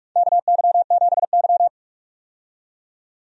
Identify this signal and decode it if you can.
Morse code, heard as a keyed tone, about 46 wpm, 700 Hz. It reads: KY7Y
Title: USA Call Signs - 4 Characters - 45wpm